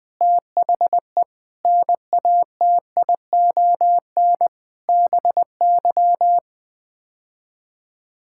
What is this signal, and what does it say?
Morse code, heard as a keyed tone, about 20 wpm, 700 Hz. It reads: THE NATION BY
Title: Sets of 9 Words - 20wpm